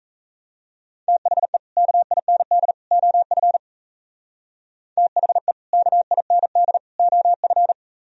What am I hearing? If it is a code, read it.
Morse code, heard as a keyed tone, about 35 wpm, 700 Hz. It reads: THE KIND OF THE KIND OF
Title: Speed-Racing - Sets of 3 Words 28wpm